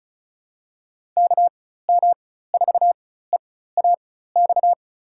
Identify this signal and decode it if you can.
Morse code, heard as a keyed tone, about 36 wpm, spaced as for 16 wpm, 700 Hz. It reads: KM4EAX
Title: USA Call Signs - 6 Characters - 4x Character Spacing - 35wpm